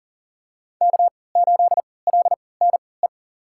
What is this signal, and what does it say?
Morse code, heard as a keyed tone, about 40 wpm, spaced as for 23 wpm, 700 Hz. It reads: K8LNE
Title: USA Call Signs - 3x Character Spacing - 40wpm